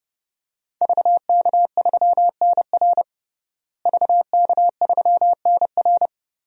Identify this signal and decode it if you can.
Morse code, heard as a keyed tone, about 30 wpm, 700 Hz. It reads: VK3NR VK3NR
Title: DXCC-100 Call Signs - 5 Characters - Repeated 3x - 30wpm